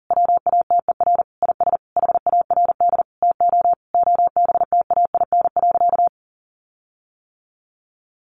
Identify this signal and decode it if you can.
Morse code, heard as a keyed tone, about 40 wpm, 700 Hz. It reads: WATER IS HARD TO OBTAIN.
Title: Sentences from Top 1300 Words 40wpm